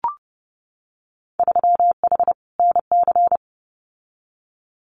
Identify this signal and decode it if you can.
Morse code, heard as a keyed tone, about 30 wpm, 700 Hz. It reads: 3H NC
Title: Winter Field Day Exchanges 30wpm